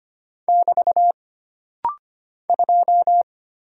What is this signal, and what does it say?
Morse code, heard as a keyed tone, about 25 wpm, 700 Hz. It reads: <BT> 2
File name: Single Character (Letter:Number:Punctuation:Prosign) - Rapid-Fire - 25wpm